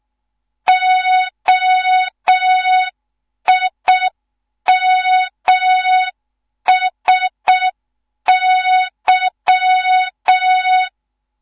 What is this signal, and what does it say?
Morse code, heard as a keyed tone, about 6 wpm, 3700 Hz. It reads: OIMSY
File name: Morze-HA2MN / hang / 14 280 HA2MN